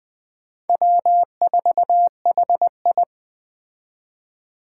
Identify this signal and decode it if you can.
Morse code, heard as a keyed tone, about 20 wpm, 700 Hz. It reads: W4HI
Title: Speed Racing - USA 1x2 Call Signs 17wpm